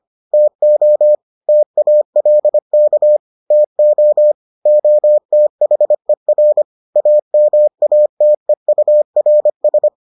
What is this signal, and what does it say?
Morse code, heard as a keyed tone, about 25 wpm, 600 Hz. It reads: TO TALK TO OTHER AMATEURS